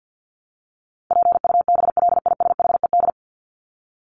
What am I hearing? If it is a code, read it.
Morse code, heard as a keyed tone, about 50 wpm, 700 Hz. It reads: PUBLISHED
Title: ED Suffix - Top 100 Words with Base Words 50wpm